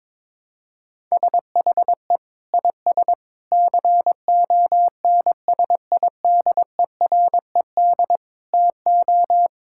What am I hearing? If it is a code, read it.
Morse code, heard as a keyed tone, about 22 wpm, 700 Hz. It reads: SHE IS CONSIDERED TO
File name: practice / 22wpm / Sentences with 8-Word Phrases - 22wpm